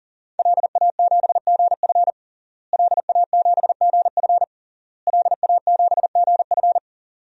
Morse code, heard as a keyed tone, about 40 wpm, 700 Hz. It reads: LA7GF LA7GF LA7GF
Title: DXCC-100 Call Signs - 5 Characters - Repeated 3x - 40wpm